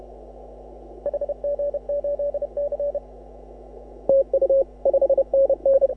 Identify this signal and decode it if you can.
Morse code, heard as a keyed tone, about 34 wpm, 530 Hz. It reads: HG8C TU 5NN
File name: AUDIO2018 / HA6OI CW